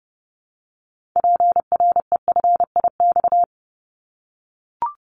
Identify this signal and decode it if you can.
Morse code, heard as a keyed tone, about 30 wpm, 700 Hz. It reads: PREFIX
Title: Callsign Prefixes for Top 50 DXCC - 30wpm